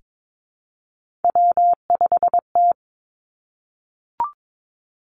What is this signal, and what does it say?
Morse code, heard as a keyed tone, about 22 wpm, 700 Hz. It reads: W5T
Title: Three Challenging Letter-Number Combinations 22wpm